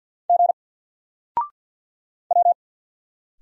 Morse code, heard as a keyed tone, about 50 wpm, 700 Hz. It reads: G W
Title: Single Letter-Number Alt 50wpm